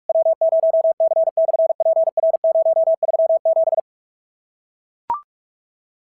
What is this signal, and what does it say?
Morse code, heard as a keyed tone, about 44 wpm, 650 Hz. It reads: W0C/PR037
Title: SOTA QSO Practice 45wpm